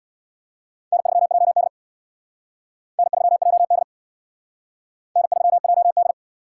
Morse code, heard as a keyed tone, about 62 wpm, 700 Hz. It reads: N3JD N3JD N3JD
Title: Speed Racing - USA 1x2 Call Signs 50wpm